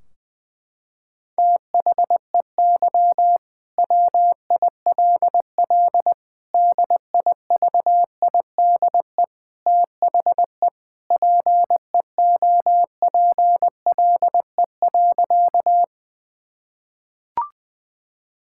Morse code, heard as a keyed tone, about 20 wpm, 700 Hz. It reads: THEY WILL DIVIDE THE PEOPLE.
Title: Sentences from Top 1600 Words 20wpm